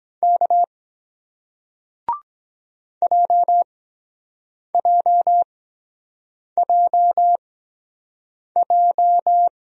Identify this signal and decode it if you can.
Morse code, heard as a keyed tone, about 24 wpm, spaced as for 7 wpm, 700 Hz. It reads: K JJJJ